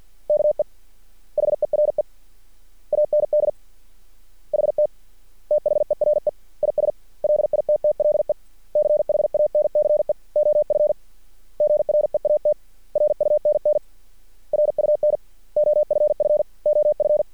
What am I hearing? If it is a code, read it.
Morse code, heard as a keyed tone, about 50 wpm, 600 Hz. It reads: KE HERE AND HT THERE IS LITTLE CHANCE OF GREAT RUNN RUN OFF OF
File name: archived / CRSnet